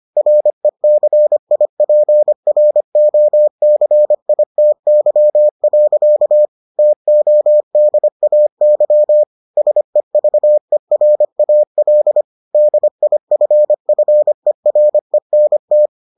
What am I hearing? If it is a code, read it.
Morse code, heard as a keyed tone, about 25 wpm, 600 Hz. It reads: RECIPROCITY. TODAY SEVERAL DIFFERENT